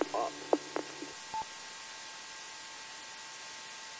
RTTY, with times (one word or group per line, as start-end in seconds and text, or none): none